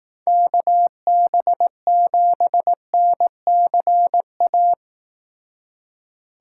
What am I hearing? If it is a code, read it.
Morse code, heard as a keyed tone, about 18 wpm, 700 Hz. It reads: KB7NCA
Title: Speed Racing - USA Call Signs 15wpm